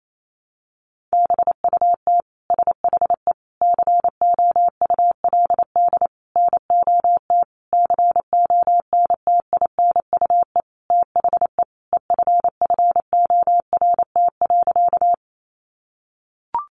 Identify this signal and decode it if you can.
Morse code, heard as a keyed tone, about 28 wpm, 700 Hz. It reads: BUT SHE COULD NOT CONTINUE THE EFFORT.
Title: Sentences from Top 1200 Words 28wpm